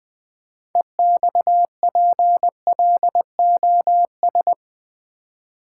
Morse code, heard as a keyed tone, about 20 wpm, 700 Hz. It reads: EXPLOS
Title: Build A Word - 3 Letters At A Time - 9-Letter Words - 20wpm